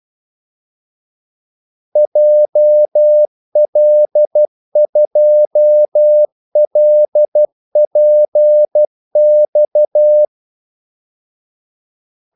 Morse code, heard as a keyed tone, about 12 wpm, 600 Hz. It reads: JL2LPX
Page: 2 ekstra ordmellemrum mellem kaldesignalerne